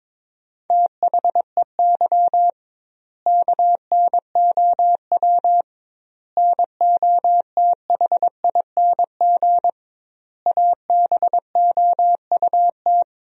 Morse code, heard as a keyed tone, about 22 wpm, 700 Hz. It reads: THEY KNOW NOTHING ABOUT
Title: 2x Word-Spacing - Sentences from Top 500 Words - 22wpm